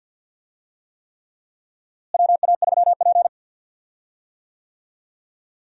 Morse code, heard as a keyed tone, about 48 wpm, 700 Hz. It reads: WA3P
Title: USA Call Signs - 4 Characters - Repeated 3x - 50wpm